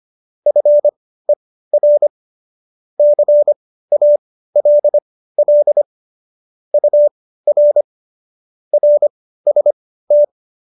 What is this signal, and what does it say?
Morse code, heard as a keyed tone, about 25 wpm, spaced as for 15 wpm, 590 Hz. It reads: FER CALL UR RST